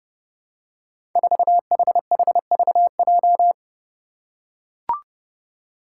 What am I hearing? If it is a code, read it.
Morse code, heard as a keyed tone, about 30 wpm, 700 Hz. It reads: 4HHVJ
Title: Five Challenging Letter-Number Combinations 30wpm